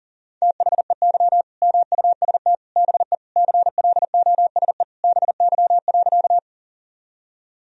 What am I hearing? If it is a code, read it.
Morse code, heard as a keyed tone, about 37 wpm, 700 Hz. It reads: THEY MUST BE CLOSE BY.
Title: Sentences from Top 1000 Words - Review Part 1 - 40wpm